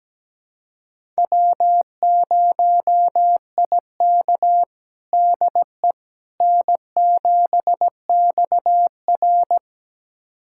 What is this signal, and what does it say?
Morse code, heard as a keyed tone, about 17 wpm, 700 Hz. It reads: W0IK DE N7XR
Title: USA Callsign DE Callsign - 4 Characters - 17wpm